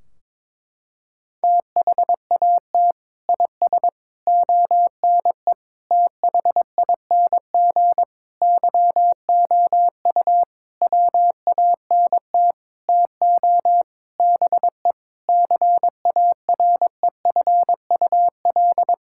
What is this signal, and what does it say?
Morse code, heard as a keyed tone, about 22 wpm, 700 Hz. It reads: THAT IS ONE THING YOU WANT TO BE CAREFUL
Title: Sentences from Top 1600 Words 22wpm